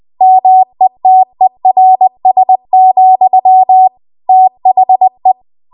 Morse code, heard as a keyed tone, about 20 wpm, 750 Hz. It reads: METERS, THE
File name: Archive / 20 WPM / 20WPM